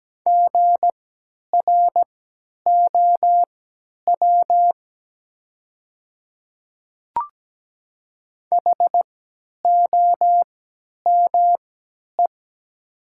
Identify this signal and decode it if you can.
Morse code, heard as a keyed tone, about 17 wpm, spaced as for 10 wpm, 700 Hz. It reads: GROW HOME
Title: 3x Character Spacing - Top 500 Words - 17wpm